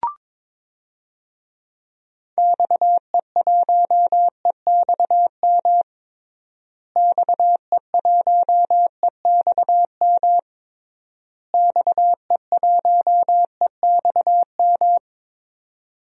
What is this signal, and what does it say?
Morse code, heard as a keyed tone, about 22 wpm, 700 Hz. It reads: XE1EXM XE1EXM XE1EXM
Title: DXCC-100 Call Signs - 6 Characters - Repeated 3x - 22wpm